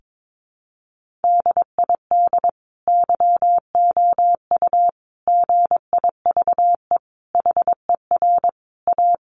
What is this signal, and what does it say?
Morse code, heard as a keyed tone, about 22 wpm, 700 Hz. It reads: DID YOU GIVE HER A
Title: Sentences from Top 1500 Words 22wpm